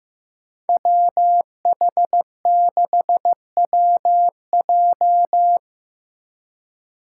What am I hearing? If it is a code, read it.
Morse code, heard as a keyed tone, about 16 wpm, 700 Hz. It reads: WH6WJ
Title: USA Call Signs - 5 Characters - 2x Character Spacing - 15wpm